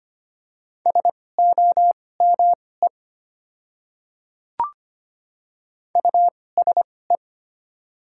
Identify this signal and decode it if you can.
Morse code, heard as a keyed tone, about 25 wpm, spaced as for 18 wpm, 700 Hz. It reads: SOME USE
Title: 2x Character Spacing - Top 100 Words - 25wpm